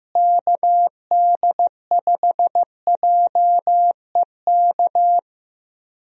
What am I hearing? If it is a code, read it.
Morse code, heard as a keyed tone, about 15 wpm, 700 Hz. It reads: KD5JEK
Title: USA Call Signs - 6 Characters - Repeated 3x - 15wpm